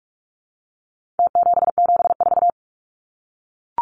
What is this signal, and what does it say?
Morse code, heard as a keyed tone, about 44 wpm, 700 Hz. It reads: T774
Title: Numbers - Contest Style - 4-Digit 45wpm